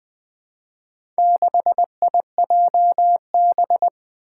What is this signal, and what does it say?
Morse code, heard as a keyed tone, about 21 wpm, 700 Hz. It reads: 6IJB
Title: Four Challenging Letter-Number Combinations 20wpm